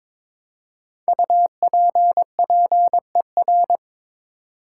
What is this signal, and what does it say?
Morse code, heard as a keyed tone, about 22 wpm, 700 Hz. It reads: UPPER
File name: 5 Letter Words - First 3 Letters Given - 22wpm